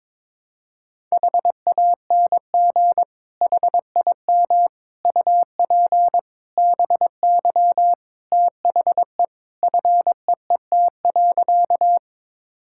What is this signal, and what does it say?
Morse code, heard as a keyed tone, about 22 wpm, 700 Hz. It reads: HANG HIM UP BY THE FEET.
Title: Sentences from Top 1600 Words 22wpm